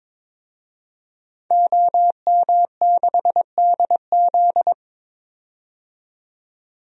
Morse code, heard as a keyed tone, about 22 wpm, 700 Hz. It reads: OM6DZ
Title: DXCC-100 Call Signs - 5 Characters - 22wpm